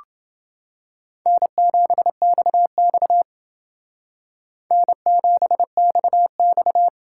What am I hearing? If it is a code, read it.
Morse code, heard as a keyed tone, about 28 wpm, 700 Hz. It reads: N7XX N7XX